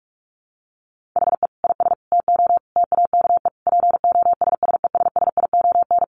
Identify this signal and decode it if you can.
Morse code, heard as a keyed tone, about 44 wpm, 700 Hz. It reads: HE IS TO TAKE POSSESSION
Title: Sentences from Top 1400 Words 45wpm